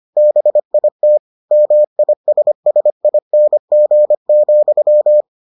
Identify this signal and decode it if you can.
Morse code, heard as a keyed tone, about 25 wpm, 600 Hz. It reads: BIT MISSING,